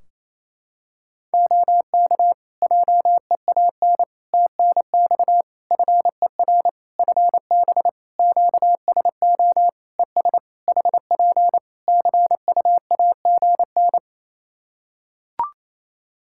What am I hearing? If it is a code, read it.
Morse code, heard as a keyed tone, about 28 wpm, 700 Hz. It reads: OK JEAN TNX FER FB QSO ES HP CUAGN
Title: Practice QSO - Exchange 4 - Ending - 28wpm